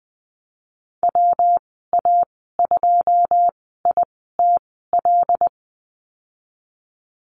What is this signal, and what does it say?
Morse code, heard as a keyed tone, about 20 wpm, spaced as for 15 wpm, 700 Hz. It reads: WA2ITL
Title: 122 Lesson 20g - Number 3 - Call Signs 20wpm